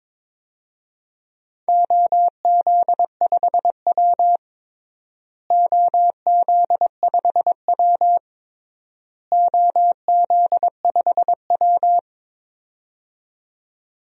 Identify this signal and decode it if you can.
Morse code, heard as a keyed tone, about 22 wpm, 700 Hz. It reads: OZ5W OZ5W OZ5W
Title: DXCC-100 Call Signs - 4 Characters - Repeated 3x - 22wpm